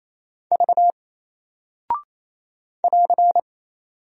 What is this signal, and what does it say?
Morse code, heard as a keyed tone, about 27 wpm, 700 Hz. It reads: V <AR>
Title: Single Character (Letter:Number:Punctuation:Prosign) 28wpm